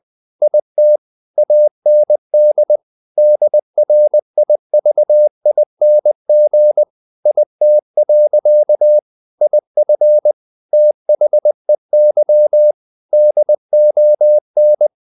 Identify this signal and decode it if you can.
Morse code, heard as a keyed tone, about 20 wpm, 600 Hz. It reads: IT AND DRIVING IT. IF THEY DON
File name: the old man and the sea / the old man and the sea part 1 20wpm